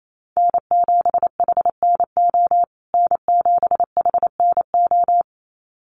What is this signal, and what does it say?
Morse code, heard as a keyed tone, about 28 wpm, 700 Hz. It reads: N7HNO N7HNO
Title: USA Call Signs Repeated 2x 28wpm